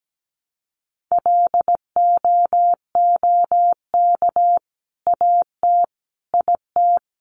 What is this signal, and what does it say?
Morse code, heard as a keyed tone, about 17 wpm, 700 Hz. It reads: LOOK AT IT
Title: Sentences from Top 200 Words 17wpm